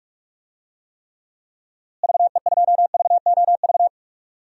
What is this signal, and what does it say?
Morse code, heard as a keyed tone, about 46 wpm, 700 Hz. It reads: VE2VOV
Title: DXCC-100 Call Signs - 6 Characters - Repeated 3x - 45wpm